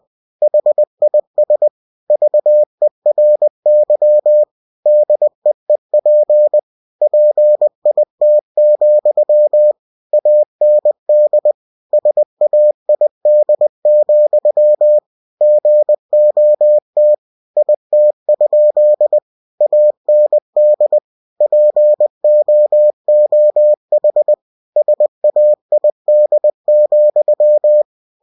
Morse code, heard as a keyed tone, about 20 wpm, 600 Hz. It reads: HIS VERY DEEP PIT, AND SAID, GOT IT? AND POOH SAID,
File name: cw-books / winnie-the-pooh / winnie-20wpm-page-0038